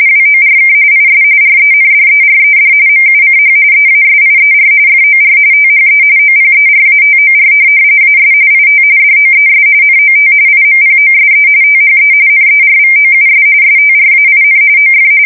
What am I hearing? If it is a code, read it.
RTTY, about 45.45 baud. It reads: WORK IS RENDERED MUCH EASIER FOR THE OPERATOR AS IT DOES NOT IMPOSE THE NERVOUS STRAIN OF
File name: fsk170 / printing-telegraph